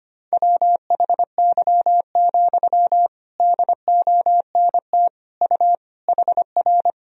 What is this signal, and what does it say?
Morse code, heard as a keyed tone, about 25 wpm, 700 Hz. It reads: WHY, DONT U HR
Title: Sentences from Top 300 Words and Abbreviations 25wpm